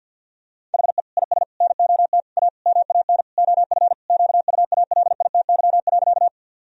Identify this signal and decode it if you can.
Morse code, heard as a keyed tone, about 50 wpm, 700 Hz. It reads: HE IS NOT A MAN OF QUALITY.
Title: Sentences from Top 1000 Words - Review Part 4 - 50wpm